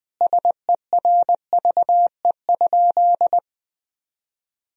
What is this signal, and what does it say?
Morse code, heard as a keyed tone, about 21 wpm, 700 Hz. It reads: SERVE?
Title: Sentences from Top 400 Words and Abbreviations 20wpm